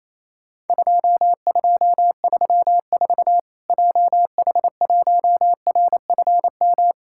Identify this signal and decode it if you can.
Morse code, heard as a keyed tone, about 28 wpm, 700 Hz. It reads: 2234 JH1RFM
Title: Worked All Europe DX Contest - QTC Exchange - 28wpm